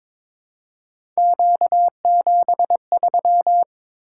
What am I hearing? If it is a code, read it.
Morse code, heard as a keyed tone, about 22 wpm, 700 Hz. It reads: Q73
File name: practice / 22wpm / Three Letter-Number Combinations - 22wpm